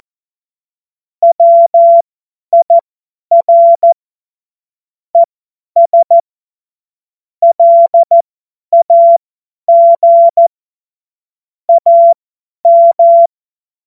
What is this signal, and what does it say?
Morse code, heard as a keyed tone, about 14 wpm, spaced as for 10 wpm, 680 Hz. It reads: WIR ES LAG AM
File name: Morsekurs / Witze / Witz 27. Juli 2021, 50 F70